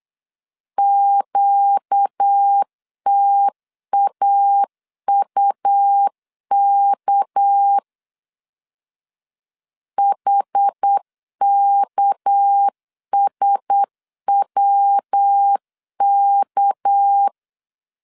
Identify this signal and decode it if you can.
Morse code, heard as a keyed tone, about 8 wpm, 800 Hz. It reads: QTAUK HKSWK
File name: LEZIONI / 6 PKX0 / 6 PK 8wpm 3003